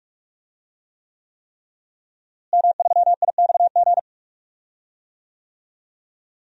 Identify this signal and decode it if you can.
Morse code, heard as a keyed tone, about 45 wpm, 700 Hz. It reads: M3IXG
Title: DXCC-100 Call Signs - 5 Characters - Repeated 3x - 45wpm